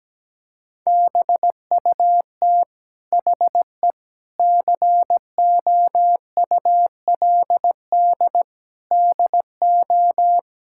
Morse code, heard as a keyed tone, about 17 wpm, 700 Hz. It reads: BUT HE COULD DO
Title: Sentences from Top 1600 Words 17wpm